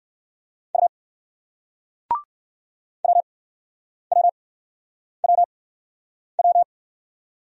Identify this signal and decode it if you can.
Morse code, heard as a keyed tone, about 60 wpm, spaced as for 9 wpm, 700 Hz. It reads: U WWWW